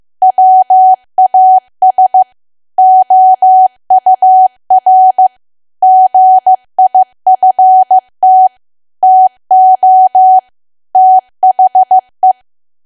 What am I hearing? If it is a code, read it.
Morse code, heard as a keyed tone, about 15 wpm, 750 Hz. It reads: WAS OUR GIFT TO THE
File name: cwa / W1AW-Level2 / 15WPM